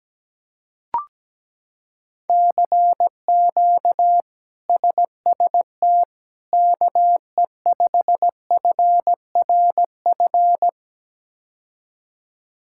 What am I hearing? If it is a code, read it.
Morse code, heard as a keyed tone, about 17 wpm, 700 Hz. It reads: CQ SST KE5FRF
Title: K1USN SST QSO Practice - 17wpm